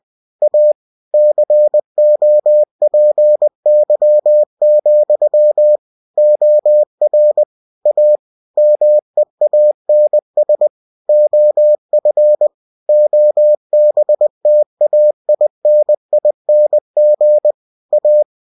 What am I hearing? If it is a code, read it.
Morse code, heard as a keyed tone, about 20 wpm, 600 Hz. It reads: A COPY, OR A MEANS OF OBTAINING A
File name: cw-books / winnie-the-pooh / winnie-20wpm-page-0116